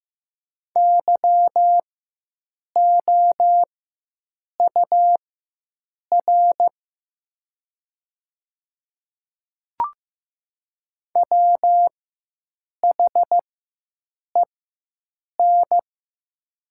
Morse code, heard as a keyed tone, about 15 wpm, spaced as for 7 wpm, 700 Hz. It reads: YOUR WHEN
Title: 4x Character Spacing - Top 200 Words - 15wpm